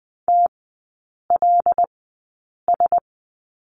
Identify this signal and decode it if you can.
Morse code, heard as a keyed tone, about 20 wpm, spaced as for 8 wpm, 700 Hz. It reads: TLS
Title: 159 Lesson 26e - Letter g - Three Characters 20wpm